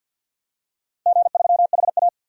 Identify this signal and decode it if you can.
Morse code, heard as a keyed tone, about 50 wpm, 700 Hz. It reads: G3HA
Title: DXCC-100 Call Signs - 4 Characters - 50wpm